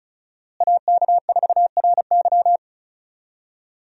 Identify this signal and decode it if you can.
Morse code, heard as a keyed tone, about 36 wpm, 700 Hz. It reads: AK4RY
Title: USA Call Signs - 5 Characters - Repeated 3x - 35wpm